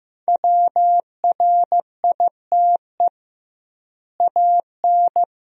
Morse code, heard as a keyed tone, about 15 wpm, 700 Hz. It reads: WRITE AN
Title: 2x Word-Spacing - Sentences from Top 300 Words - 15wpm